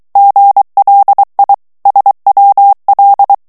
Morse code, heard as a keyed tone, about 23 wpm, 800 Hz. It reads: GLI SWL
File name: ITALIEN texte clair 2 vitesse 22 mots minute